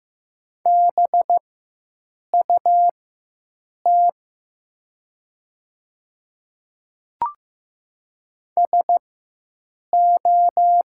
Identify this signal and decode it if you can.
Morse code, heard as a keyed tone, about 15 wpm, spaced as for 7 wpm, 700 Hz. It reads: BUT SO
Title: 4x Character Spacing - Top 100 Words - 15wpm